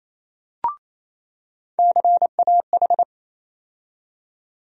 Morse code, heard as a keyed tone, about 28 wpm, 700 Hz. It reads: CAH
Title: USA Call Sign Postfixes - 3 Letters 28wpm